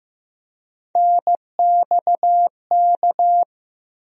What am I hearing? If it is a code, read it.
Morse code, heard as a keyed tone, about 15 wpm, 700 Hz. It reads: NXK
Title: USA Call Sign Postfixes - 3 Letters 15wpm